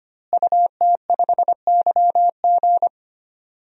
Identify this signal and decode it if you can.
Morse code, heard as a keyed tone, about 25 wpm, 700 Hz. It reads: UT5YG
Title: DXCC-100 Call Signs - 5 Characters - Repeated 3x - 25wpm